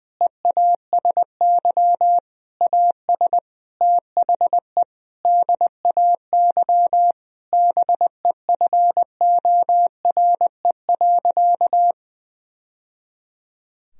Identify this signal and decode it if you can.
Morse code, heard as a keyed tone, about 20 wpm, 700 Hz. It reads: EASY AS THE DAY BEFORE.
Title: Sentences from Top 1000 Words - Review Part 1 - 20wpm